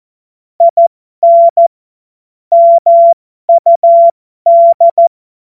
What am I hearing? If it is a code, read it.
Morse code, heard as a keyed tone, about 14 wpm, spaced as for 12 wpm, 680 Hz. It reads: IN MUD
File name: Morsekurs / Witze